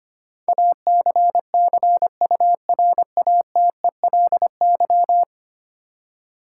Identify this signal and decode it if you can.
Morse code, heard as a keyed tone, about 25 wpm, 700 Hz. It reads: ACCURATELY
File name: Ten Letter Words - 25wpm